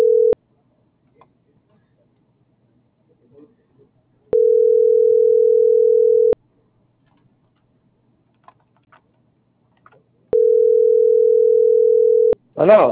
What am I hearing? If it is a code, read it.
Ringback tone (North America), 440 + 480 Hz.